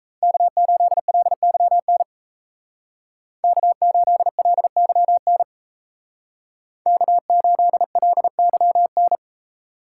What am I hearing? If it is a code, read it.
Morse code, heard as a keyed tone, about 34 wpm, 700 Hz. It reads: K8LYN K8LYN K8LYN
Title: Speed Racing - USA 1x3 Call Signs 28wpm